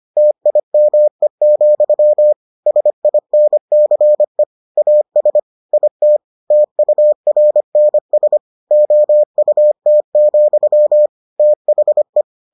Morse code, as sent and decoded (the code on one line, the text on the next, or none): - .. -- . --..-- / ... .. -. -.-. / .- ... / .. - / - ..- .-. -. ... / --- ..- - --..-- / - .... .
TIME, SINCE AS IT TURNS OUT, THE